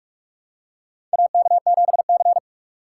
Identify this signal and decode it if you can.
Morse code, heard as a keyed tone, about 43 wpm, 700 Hz. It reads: AK7C
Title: USA Call Signs - 6x Character Spacing - 45wpm